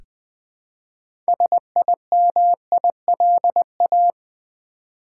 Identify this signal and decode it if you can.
Morse code, heard as a keyed tone, about 20 wpm, 700 Hz. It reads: SIMILA
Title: Build A Word - 3 Letters At A Time - 9-Letter Words - 20wpm